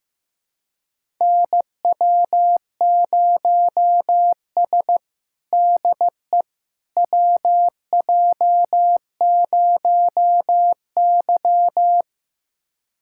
Morse code, heard as a keyed tone, about 16 wpm, 700 Hz. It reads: NW0S DE WJ0Y
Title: USA Callsign DE Callsign - 4 Characters - 15wpm